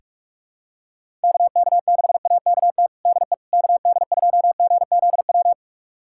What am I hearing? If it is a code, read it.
Morse code, heard as a keyed tone, about 44 wpm, 700 Hz. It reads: KK6AKT DE KD2GZW
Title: USA Callsign DE Callsign - 6 Characters - 45wpm